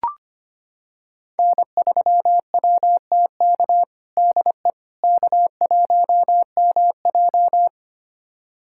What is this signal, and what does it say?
Morse code, heard as a keyed tone, about 25 wpm, 700 Hz. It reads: N3WTK DE K1MJ
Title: USA Callsign DE Callsign 25wpm